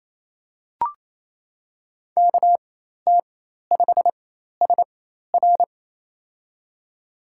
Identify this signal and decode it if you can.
Morse code, heard as a keyed tone, about 28 wpm, spaced as for 13 wpm, 700 Hz. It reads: KT5SR
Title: USA Call Signs - 5 Characters - 4x Character Spacing - 28wpm